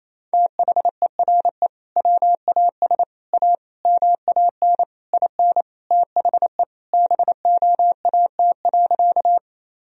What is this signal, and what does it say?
Morse code, heard as a keyed tone, about 28 wpm, 700 Hz. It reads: THERE WAS A MAN IN THE BOAT.
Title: Sentences from Top 1000 Words - Review Part 3 - 28wpm